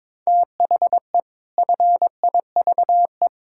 Morse code, heard as a keyed tone, about 22 wpm, 700 Hz. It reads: THE FIVE
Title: Pangram Challenge v2 - 22wpm